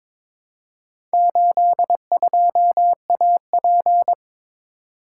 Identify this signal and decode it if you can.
Morse code, heard as a keyed tone, about 22 wpm, 700 Hz. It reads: 82AP
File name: practice / 22wpm / Four Letter-Number Combinations - 22wpm